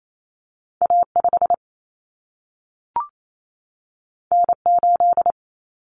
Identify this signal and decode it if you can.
Morse code, heard as a keyed tone, about 28 wpm, 700 Hz. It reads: A5 N8
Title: Common Cut Numbers - 2-Digit 28wpm